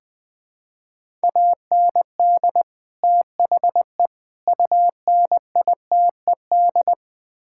Morse code, heard as a keyed tone, about 20 wpm, 700 Hz. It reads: AND THE UNITED
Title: Build A Phrase - 4-Word Phrases - 20wpm